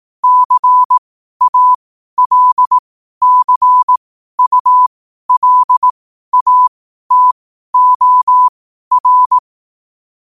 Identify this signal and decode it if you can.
Morse code, heard as a keyed tone, about 18 wpm, spaced as for 12 wpm, 1000 Hz. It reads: CALCULATOR
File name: qotdfiles / 12wpm